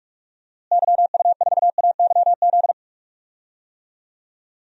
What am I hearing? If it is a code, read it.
Morse code, heard as a keyed tone, about 46 wpm, 700 Hz. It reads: YU4AYZ
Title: DXCC-100 Call Signs - 6 Characters - Repeated 3x - 45wpm